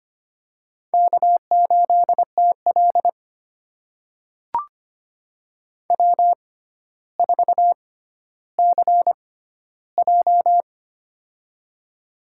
Morse code, heard as a keyed tone, about 25 wpm, spaced as for 13 wpm, 700 Hz. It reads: K8TL W4CJ
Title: USA Call Signs - 4 Characters - 6x Character Spacing - 25wpm